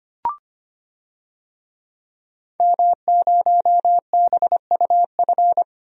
Morse code, heard as a keyed tone, about 25 wpm, 700 Hz. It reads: M0BUF
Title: DXCC-100 Call Signs - 5 Characters - 25wpm